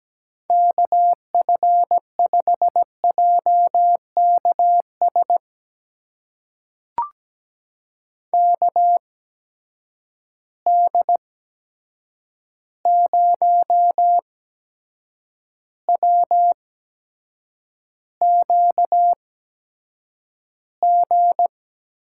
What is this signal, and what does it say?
Morse code, heard as a keyed tone, about 17 wpm, spaced as for 7 wpm, 700 Hz. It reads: KF5JKS KD0WQG
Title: USA Call Signs - 6 Characters - 8x Character Spacing - 17wpm